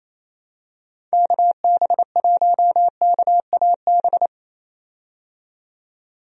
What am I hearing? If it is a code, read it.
Morse code, heard as a keyed tone, about 27 wpm, 700 Hz. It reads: KB1KAB
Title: USA Call Signs - 6 Characters - 28wpm